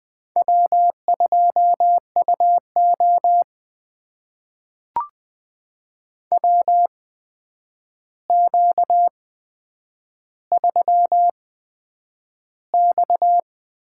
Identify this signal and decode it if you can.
Morse code, heard as a keyed tone, about 20 wpm, spaced as for 9 wpm, 700 Hz. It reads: W2UO WQ3X
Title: USA Call Signs - 4 Characters - 8x Character Spacing - 20wpm